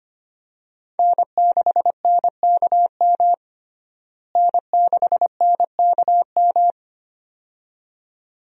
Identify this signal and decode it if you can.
Morse code, heard as a keyed tone, about 25 wpm, 700 Hz. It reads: N6NKM N6NKM
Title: USA Call Signs - 5 Characters - Repeated 2x - 25wpm